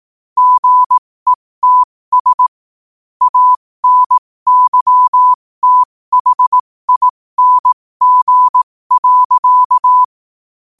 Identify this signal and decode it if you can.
Morse code, heard as a keyed tone, about 18 wpm, spaced as for 15 wpm, 1000 Hz. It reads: GETS ANYTHING.
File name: qotdfiles / 15wpm